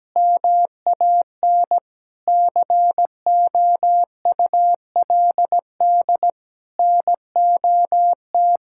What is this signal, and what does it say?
Morse code, heard as a keyed tone, about 17 wpm, 700 Hz. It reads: MAN COULD NOT
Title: Sentences from Top 1600 Words 17wpm